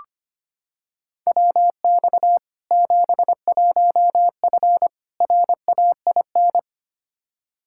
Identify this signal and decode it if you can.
Morse code, heard as a keyed tone, about 25 wpm, 700 Hz. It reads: WX 71F RAIN
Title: K1USN SST QSO Practice 25wpm